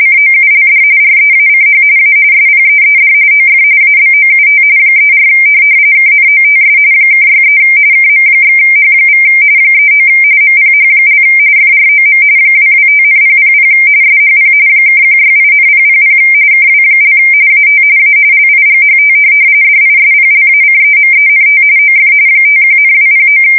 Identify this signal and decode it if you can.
RTTY, about 45.45 baud. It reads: COULD REACH A SPEED OF ABOUT 34 KNOTS (63 KM/H= 39 MPH) FROM 225,000 SHP (168,000 KW) AT A LIGHT DISPLACEMENT OF 51,209
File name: fsk170 / USS-Missouri